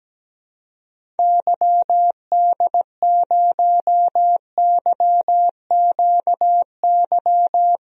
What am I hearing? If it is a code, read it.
Morse code, heard as a keyed tone, about 17 wpm, 700 Hz. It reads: YD0YQY
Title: DXCC-100 Call Signs - 6 Characters - Repeated 2x - 17wpm